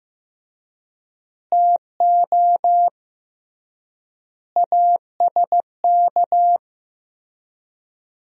Morse code, heard as a keyed tone, about 15 wpm, 700 Hz. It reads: TO ASK
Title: 3x Word-Spacing - Sentences from Top 200 Words - 15wpm